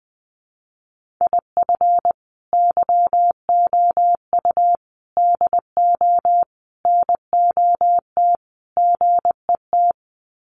Morse code, heard as a keyed tone, about 19 wpm, 700 Hz. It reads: IF YOU DO NOT GET
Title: Sets of 5 Words Encore2 20wpm